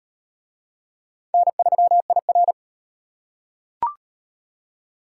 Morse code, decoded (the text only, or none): N3IR